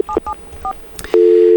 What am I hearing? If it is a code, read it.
DTMF, heard as telephone touch tones, tones of about 80 ms. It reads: 774